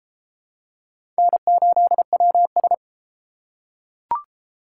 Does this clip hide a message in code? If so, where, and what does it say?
Yes: Morse code from 1.2 to 2.7 s, heard as a keyed tone, about 32 wpm, 700 Hz. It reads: N8WS